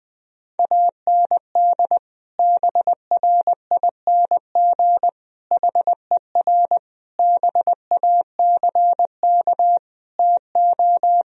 Morse code, heard as a keyed tone, about 20 wpm, 700 Hz. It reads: AND BRING HER BACK TO
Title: Sentences from Top 1000 Words - Review Part 3 - 20wpm